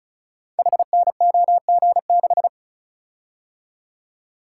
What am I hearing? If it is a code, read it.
Morse code, heard as a keyed tone, about 36 wpm, 700 Hz. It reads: HNOG6
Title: Code Groups - 5 Letter:Numbers - 35wpm